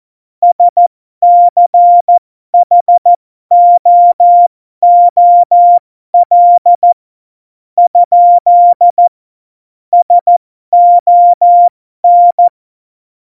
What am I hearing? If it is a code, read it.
Morse code, heard as a keyed tone, about 14 wpm, spaced as for 12 wpm, 700 Hz. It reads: SCHOOL ? SON
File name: Morsekurs / Witze